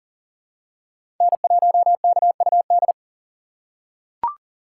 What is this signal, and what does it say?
Morse code, heard as a keyed tone, about 40 wpm, 700 Hz. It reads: N1KUD
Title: USA Call Signs - 5 Characters - 4x Character Spacing - 40wpm